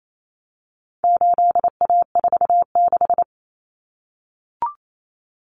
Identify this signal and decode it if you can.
Morse code, heard as a keyed tone, about 28 wpm, 700 Hz. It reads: 8A46
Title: Common Cut Numbers - 4-Digit 28wpm